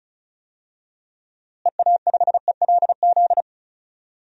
Morse code, heard as a keyed tone, about 34 wpm, 700 Hz. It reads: EA5ELZ
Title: DXCC-100 Call Signs - 6 Characters - Repeated 3x - 35wpm